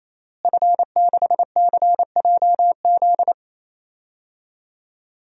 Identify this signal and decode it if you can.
Morse code, heard as a keyed tone, about 27 wpm, 700 Hz. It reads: F6CJZ
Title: DXCC-100 Call Signs - 5 Characters - Repeated 3x - 28wpm